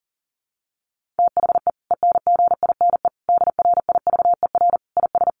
Morse code, heard as a keyed tone, about 37 wpm, 700 Hz. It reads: THE ENGINE DRIVER IS
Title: Sentences from Top 1500 Words 40wpm